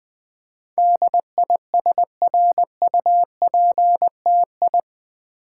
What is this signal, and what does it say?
Morse code, heard as a keyed tone, about 20 wpm, 700 Hz. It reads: DISRUPTI
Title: Build A Word - 4 Letters At A Time - 10-Letter Words - 20wpm